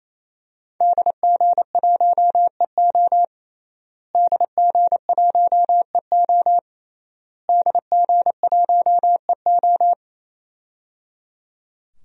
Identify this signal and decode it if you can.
Morse code, heard as a keyed tone, about 28 wpm, 700 Hz. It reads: DG1EO DG1EO DG1EO
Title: DXCC-100 Call Signs - 5 Characters - Repeated 3x - 28wpm